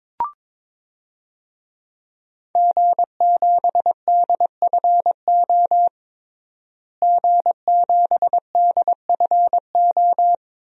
Morse code, heard as a keyed tone, about 22 wpm, 700 Hz. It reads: G7DFO G7DFO
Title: DXCC-100 Call Signs - 5 Characters - Repeated 2x - 22wpm